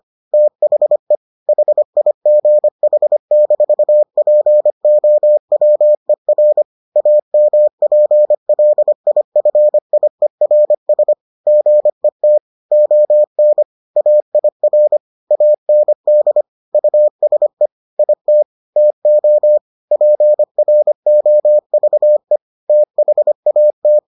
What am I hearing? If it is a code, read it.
Morse code, heard as a keyed tone, about 25 wpm, 600 Hz. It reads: THE HIGH-POWER AMPLIFIERS GET ON AIR AND USE IT TO PROVE THAT